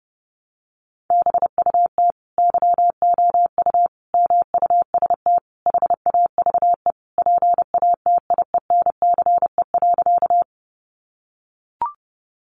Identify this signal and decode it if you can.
Morse code, heard as a keyed tone, about 30 wpm, 700 Hz. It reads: BUT YOU MUST HAVE PATIENCE.
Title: Sentences from Top 1300 Words 30wpm